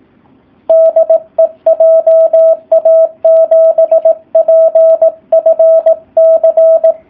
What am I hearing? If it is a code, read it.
Morse code, heard as a keyed tone, about 18 wpm, 650 Hz, with radio noise. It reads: DE JA7PFC
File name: jh8cbh / 2019cw / SP04- w